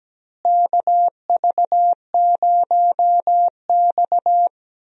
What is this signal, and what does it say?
Morse code, heard as a keyed tone, about 17 wpm, 700 Hz. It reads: KV0X
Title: USA Call Signs - 4 Characters - Repeated 2x - 17wpm